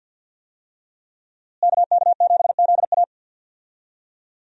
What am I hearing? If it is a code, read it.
Morse code, heard as a keyed tone, about 52 wpm, 700 Hz. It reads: KK7ZA
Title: USA Call Signs - 5 Characters - Repeated 3x - 50wpm